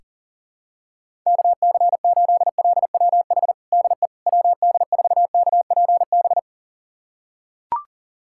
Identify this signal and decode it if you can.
Morse code, heard as a keyed tone, about 40 wpm, 700 Hz. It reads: KC8LWH DE WD4KPB
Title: USA Callsign DE Callsign 40wpm